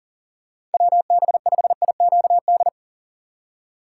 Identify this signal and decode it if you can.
Morse code, heard as a keyed tone, about 40 wpm, 700 Hz. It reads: WB5IQD
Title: USA Call Signs - 6 Characters - 3x Character Spacing - 40wpm